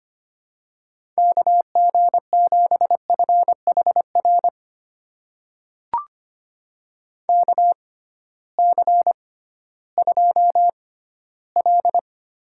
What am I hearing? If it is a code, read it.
Morse code, heard as a keyed tone, about 25 wpm, spaced as for 15 wpm, 700 Hz. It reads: KG7FHR KC2L
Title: USA Call Signs - 6 Characters - 6x Character Spacing - 25wpm